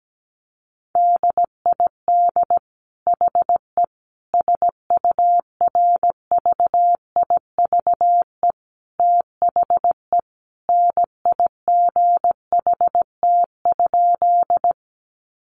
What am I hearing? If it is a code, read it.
Morse code, heard as a keyed tone, about 17 wpm, 700 Hz. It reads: DID HE SURVIVE THE NIGHT?
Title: Sentences from Top 1300 Words 17wpm